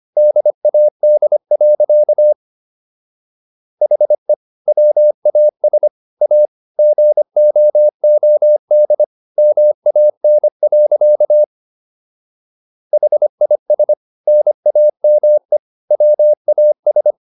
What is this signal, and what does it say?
Morse code, heard as a keyed tone, about 25 wpm, 600 Hz. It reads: DAD. HE WAS A GOOD MAN. HIS NAME WAS